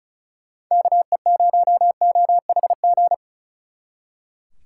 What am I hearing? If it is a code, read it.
Morse code, heard as a keyed tone, about 33 wpm, 700 Hz. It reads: KE0OHG
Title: Speed Racing - USA 2x3 Call Signs 35wpm